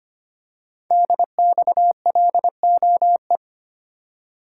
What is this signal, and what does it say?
Morse code, heard as a keyed tone, about 25 wpm, 700 Hz. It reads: DXLOE
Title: Code Groups - 5 Letters - 25wpm